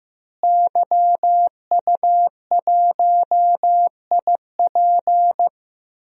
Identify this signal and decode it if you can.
Morse code, heard as a keyed tone, about 15 wpm, 700 Hz. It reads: YU1IP
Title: DXCC-100 Call Signs - 5 Characters - Repeated 3x - 15wpm